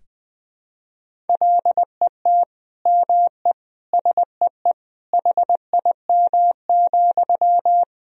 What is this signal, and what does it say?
Morse code, heard as a keyed tone, about 20 wpm, 700 Hz. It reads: LET ME SEE HIM,
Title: Sentences from Top 400 Words and Abbreviations 20wpm